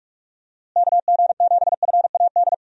Morse code, heard as a keyed tone, about 46 wpm, 700 Hz. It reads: KG7FAD
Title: USA Call Signs - 6 Characters - Repeated 2x - 45wpm